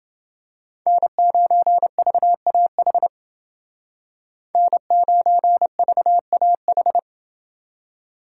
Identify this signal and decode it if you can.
Morse code, heard as a keyed tone, about 28 wpm, 700 Hz. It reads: N9VAH N9VAH